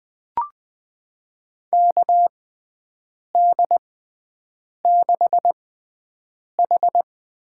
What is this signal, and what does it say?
Morse code, heard as a keyed tone, about 21 wpm, 700 Hz. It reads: K D 6 H
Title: USA Call Signs - 6 Characters - 6x Character Spacing - 20wpm